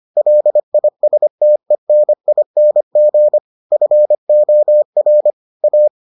Morse code, heard as a keyed tone, about 25 wpm, 600 Hz. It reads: LISTENING FOR A